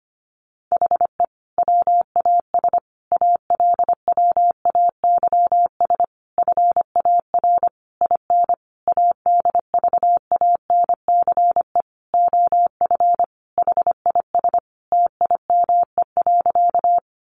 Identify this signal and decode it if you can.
Morse code, heard as a keyed tone, about 25 wpm, 700 Hz. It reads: HE WAS ALWAYS FAR IN ADVANCE OF HIS TIME.
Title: Sentences from Top 1400 Words 25wpm